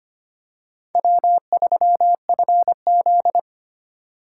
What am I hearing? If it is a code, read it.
Morse code, heard as a keyed tone, about 25 wpm, 700 Hz. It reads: W3FZ
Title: USA Call Signs - 4 Characters - 2x Character Spacing - 25wpm